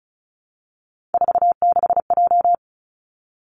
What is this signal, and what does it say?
Morse code, heard as a keyed tone, about 36 wpm, 700 Hz. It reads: 46J
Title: Three Challenging Letter-Number Combinations 35wpm